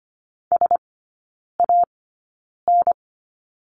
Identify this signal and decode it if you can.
Morse code, heard as a keyed tone, about 25 wpm, spaced as for 9 wpm, 700 Hz. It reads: SAN
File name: Lesson 17e - Letter u - Three Characters 25wpm